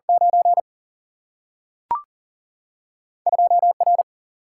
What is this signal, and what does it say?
Morse code, heard as a keyed tone, about 40 wpm, 700 Hz. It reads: 9 2 R
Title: Single-Letter Single-Number 40wpm